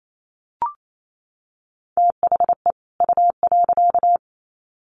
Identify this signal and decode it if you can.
Morse code, heard as a keyed tone, about 28 wpm, 700 Hz. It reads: THE U.
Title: Sets of 2 Words 28wpm